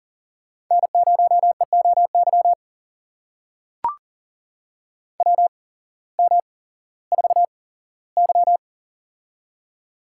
Morse code, heard as a keyed tone, about 38 wpm, spaced as for 18 wpm, 700 Hz. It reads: N0EOY WM4Y
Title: USA Call Signs - 8x Character Spacing - 40wpm